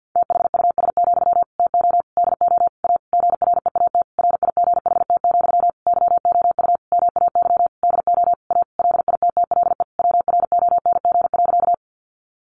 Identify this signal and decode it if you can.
Morse code, heard as a keyed tone, about 50 wpm, 700 Hz. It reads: THUS, TO DO A GREAT RIGHT, YOU MAY DO A LITTLE WRONG.
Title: Sentences from Top 900 Words 50wpm